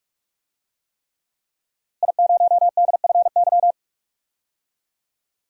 Morse code, heard as a keyed tone, about 46 wpm, 700 Hz. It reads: I0DFY
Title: DXCC-100 Call Signs - 5 Characters - Repeated 3x - 45wpm